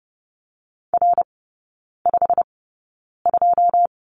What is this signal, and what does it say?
Morse code, heard as a keyed tone, about 30 wpm, spaced as for 9 wpm, 700 Hz. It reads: R52
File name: Lesson 19e - Letter w - Three Characters 30wpm